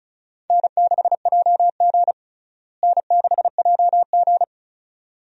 Morse code, heard as a keyed tone, about 36 wpm, 700 Hz. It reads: N6JG N6JG
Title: USA Call Signs - 4 Characters - Repeated 2x - 35wpm